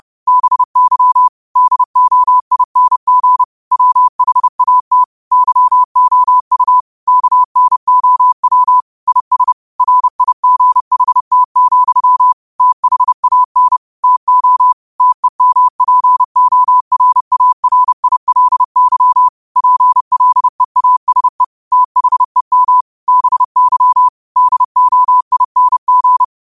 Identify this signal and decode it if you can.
Morse code, heard as a keyed tone, about 30 wpm, 1000 Hz. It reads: BY DOING WHAT YOU KNOW IS RIGHT, THAN TO TEMPORARILY PLEASE THEM BY DOING
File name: qotdfiles / 30wpm